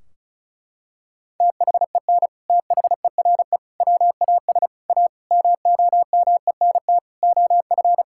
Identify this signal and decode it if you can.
Morse code, heard as a keyed tone, about 34 wpm, 700 Hz. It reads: THEN THERE WAS A MOMENT OF
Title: Sentences from Top 1600 Words 35wpm